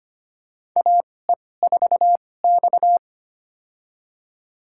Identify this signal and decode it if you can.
Morse code, heard as a keyed tone, about 25 wpm, spaced as for 18 wpm, 700 Hz. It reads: AE4X
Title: USA Call Signs - 4 Characters - 2x Character Spacing - 25wpm